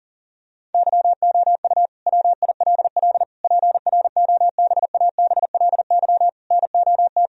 Morse code, heard as a keyed tone, about 37 wpm, 700 Hz. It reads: YOU WILL PROBABLY NOT
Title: Sentences from Top 1000 Words - Review Part 4 - 40wpm